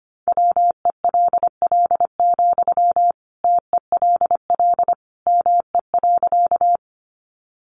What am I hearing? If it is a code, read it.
Morse code, heard as a keyed tone, about 25 wpm, 700 Hz. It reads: WELL, TELL ME.
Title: Sentences from Top 200 Words 25wpm